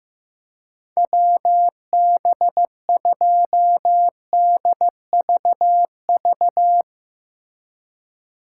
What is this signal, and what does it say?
Morse code, heard as a keyed tone, about 16 wpm, 700 Hz. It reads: WB2DVV
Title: Speed Racing - USA 2x3 Call Signs 15wpm